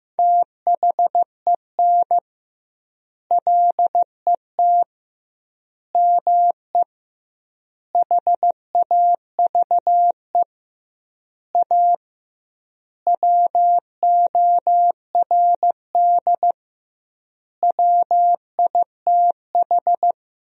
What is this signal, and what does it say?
Morse code, heard as a keyed tone, about 16 wpm, 700 Hz. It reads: THEN LET ME HAVE A WORD WITH
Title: 2x Word-Spacing - Sentences from Top 300 Words - 15wpm